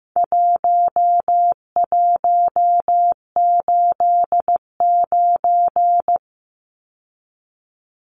Morse code, heard as a keyed tone, about 15 wpm, 700 Hz. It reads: 1189
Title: Numbers - 4-Digit 15wpm